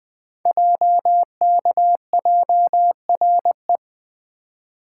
Morse code, heard as a keyed tone, about 20 wpm, 700 Hz. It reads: JKJRE
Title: Code Groups - 5 Letters - 20wpm